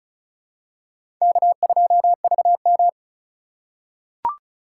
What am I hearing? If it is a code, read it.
Morse code, heard as a keyed tone, about 34 wpm, 690 Hz. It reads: K2VM
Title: USA Call Signs - 3x Character Spacing - 35wpm